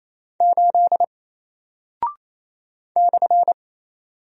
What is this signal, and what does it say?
Morse code, heard as a keyed tone, about 27 wpm, 700 Hz. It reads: 8 /
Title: Single Character (Letter:Number:Punctuation:Prosign) 28wpm